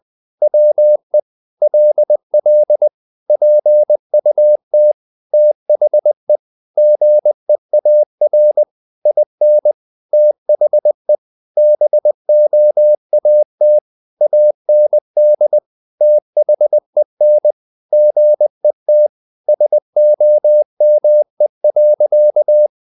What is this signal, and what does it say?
Morse code, heard as a keyed tone, about 20 wpm, 600 Hz. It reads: WE LL PUT THE GEAR IN THE BOAT AND THEN GET SOME.
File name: the old man and the sea / the old man and the sea part 1 20wpm